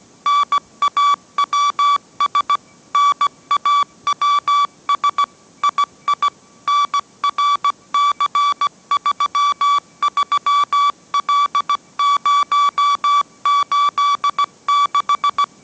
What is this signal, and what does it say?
Morse code, heard as a keyed tone, about 17 wpm, 1150 Hz. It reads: NAWS NAWS II NRC33L086
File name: mrx / mrx080 18wpm